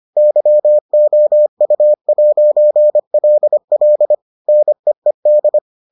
Morse code, heard as a keyed tone, about 25 wpm, 600 Hz. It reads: YOU'LL NEED